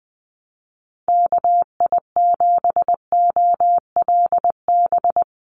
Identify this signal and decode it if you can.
Morse code, heard as a keyed tone, about 20 wpm, 700 Hz. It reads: KI7OLB
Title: USA Call Signs 20wpm